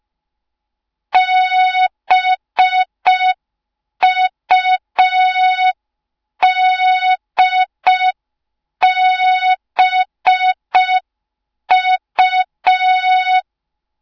Morse code, heard as a keyed tone, about 5 wpm, 3700 Hz. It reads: BUDBU